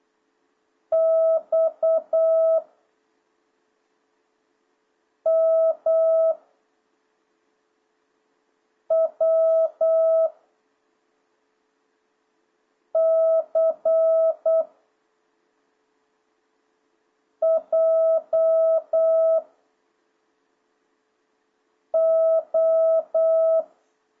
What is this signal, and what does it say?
Morse code, heard as a keyed tone, about 8 wpm, 650 Hz. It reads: X M W C J O